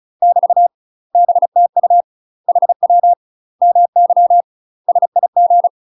Morse code, heard as a keyed tone, about 36 wpm, 700 Hz. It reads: <BT> BTU HW MY SIG
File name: UK MASTER QSOs TRAINING / 35WPM / UK MASTER QSOs 0017